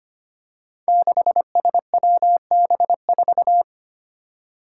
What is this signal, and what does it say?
Morse code, heard as a keyed tone, about 25 wpm, 700 Hz. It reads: 6SWB4
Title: Five Challenging Letter-Number Combinations 25wpm